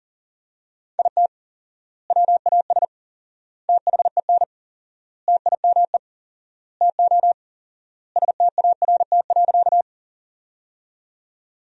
Morse code, heard as a keyed tone, about 40 wpm, 700 Hz. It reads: IT WAS THEN TIME TO START.
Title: 4x Word-Spacing - Sentences from Top 300 Words - 40wpm